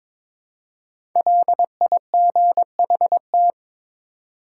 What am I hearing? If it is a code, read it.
Morse code, heard as a keyed tone, about 23 wpm, 700 Hz. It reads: LIGHT
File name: 5 Letter Words - First 3 Letters Given - 22wpm